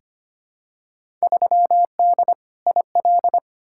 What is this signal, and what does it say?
Morse code, heard as a keyed tone, about 25 wpm, 700 Hz. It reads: 3D IL
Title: Field Day Exchanges Encore - 25wpm